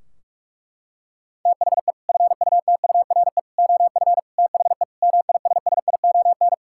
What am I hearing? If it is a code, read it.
Morse code, heard as a keyed tone, about 46 wpm, 700 Hz. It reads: THE FUTURE OF THE MISSION
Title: Sentences from Top 1600 Words 45wpm